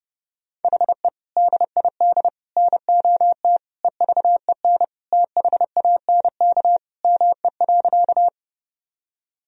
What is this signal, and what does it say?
Morse code, heard as a keyed tone, about 30 wpm, 700 Hz. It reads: HE DID NOT EVEN THANK ME.
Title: Sentences from Top 1000 Words - Review Part 1 - 30wpm